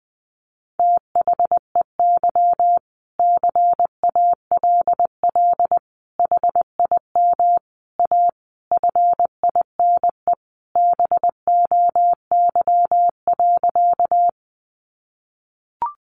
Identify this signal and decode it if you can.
Morse code, heard as a keyed tone, about 20 wpm, 700 Hz. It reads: THEY CALL HIM A FINE BOY.
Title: Sentences from Top 500 Words 20wpm